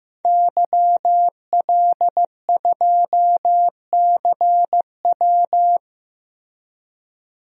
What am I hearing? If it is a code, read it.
Morse code, heard as a keyed tone, about 15 wpm, 700 Hz. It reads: YL2CW
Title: DXCC-100 Call Signs - 5 Characters - 15wpm